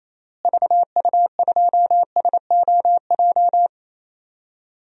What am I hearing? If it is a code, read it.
Morse code, heard as a keyed tone, about 28 wpm, 700 Hz. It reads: VU2SOJ
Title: DXCC-100 Call Signs - 6 Characters - Repeated 2x - 28wpm